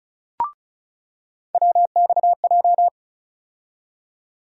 Morse code, heard as a keyed tone, about 36 wpm, 700 Hz. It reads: WXJ
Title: USA Call Sign Postfixes - 3 Letters 35wpm